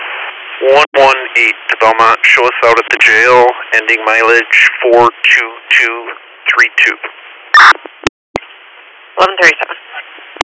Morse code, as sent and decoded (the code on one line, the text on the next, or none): none